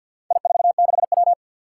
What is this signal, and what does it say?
Morse code, heard as a keyed tone, about 48 wpm, 700 Hz. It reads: I46W
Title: Four Challenging Letter-Number Combinations 50wpm